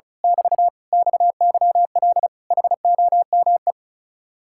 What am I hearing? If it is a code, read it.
Morse code, heard as a keyed tone, about 33 wpm, 700 Hz. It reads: <BT> XYL HOME
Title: Speed-Racing Common QSO Phrases - 35wpm